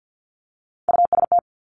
Morse code, heard as a keyed tone, about 50 wpm, 700 Hz. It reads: UST
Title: Three Challenging Letter-Number Combinations 50wpm